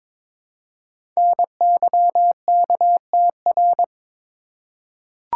Morse code, heard as a keyed tone, about 22 wpm, 700 Hz. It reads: NYKTR
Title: Code Groups - 5 Letters - 22wpm